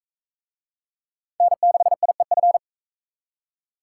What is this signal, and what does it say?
Morse code, heard as a keyed tone, about 36 wpm, 700 Hz. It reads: N6IEF